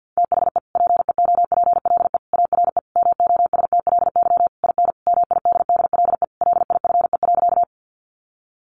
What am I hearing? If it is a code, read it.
Morse code, heard as a keyed tone, about 50 wpm, 700 Hz. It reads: THE PEOPLE ARE MOSTLY IN MIDDLE LIFE.
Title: Sentences from Top 1300 Words 50wpm